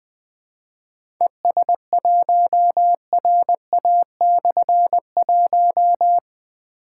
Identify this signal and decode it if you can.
Morse code, heard as a keyed tone, about 20 wpm, 700 Hz. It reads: ES1RA/1
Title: Difficult Call Signs 20wpm